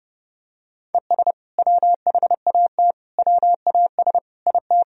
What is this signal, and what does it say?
Morse code, heard as a keyed tone, about 29 wpm, 700 Hz. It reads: ES WHAT WAS IT
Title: Sentences from Top 100 Words and Abbreviations 30wpm